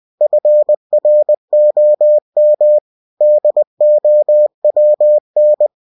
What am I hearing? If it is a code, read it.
Morse code, heard as a keyed tone, about 20 wpm, 600 Hz. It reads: FROM DOWN